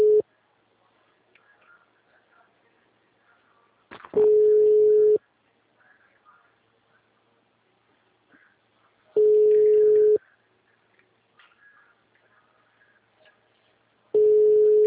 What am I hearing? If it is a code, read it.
Ringback tone (Europe, 425 Hz).
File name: out- -2018- - - .96